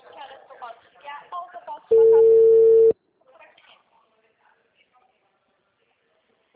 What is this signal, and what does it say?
Ringback tone (China, 450 Hz), with speech.
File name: out- -2009- - - .74